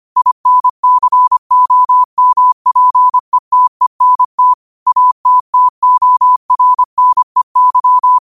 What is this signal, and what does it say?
Morse code, heard as a keyed tone, about 25 wpm, 1000 Hz. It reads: INCOMPETENT ATTORNEY
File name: qotdfiles / 25wpm